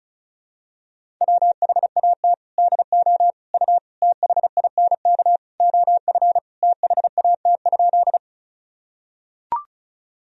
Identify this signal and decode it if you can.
Morse code, heard as a keyed tone, about 34 wpm, 700 Hz. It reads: WHAT DO U THINK OF THAT?
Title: Sentences from Top 200 Words and Abbreviations 35wpm